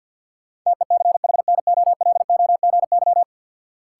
Morse code, heard as a keyed tone, about 52 wpm, 700 Hz. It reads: TECHNOLOGY
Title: Sentences from Top 1000 Words - Review Part 5 - New Word Drill - 50wpm